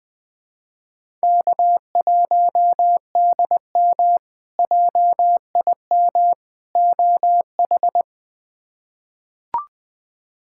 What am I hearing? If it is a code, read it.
Morse code, heard as a keyed tone, about 21 wpm, 700 Hz. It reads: K1DM JIM OH
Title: K1USN SST QSO Practice - 20wpm